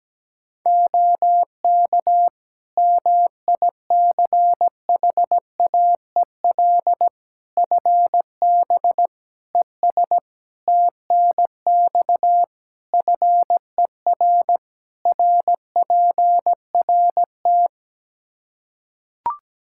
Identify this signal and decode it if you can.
Morse code, heard as a keyed tone, about 17 wpm, 700 Hz. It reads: OK MICHAEL FB ES TNX FER RPRT
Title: K1USN SST QSO Practice 17wpm